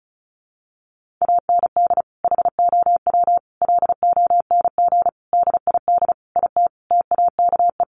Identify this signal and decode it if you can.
Morse code, heard as a keyed tone, about 34 wpm, 700 Hz. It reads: AND HOW LONG DID IT TAKE
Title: Sentences from Top 1500 Words 35wpm